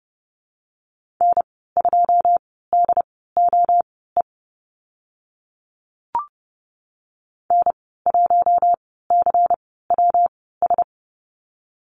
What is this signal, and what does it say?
Morse code, heard as a keyed tone, about 30 wpm, spaced as for 17 wpm, 700 Hz. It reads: N2DOE N1CWS
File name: Lesson 19g - Letter w - Call Signs 30wpm